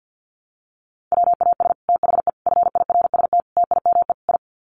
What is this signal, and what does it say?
Morse code, heard as a keyed tone, about 50 wpm, 700 Hz. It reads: WAS THE FIRST TIME I
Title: Sets of 5 Words Encore 50wpm